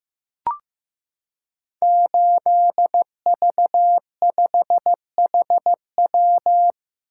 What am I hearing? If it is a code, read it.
Morse code, heard as a keyed tone, about 15 wpm, 700 Hz. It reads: 8V5HW
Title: Code Groups - 5 Letter:Numbers - 15wpm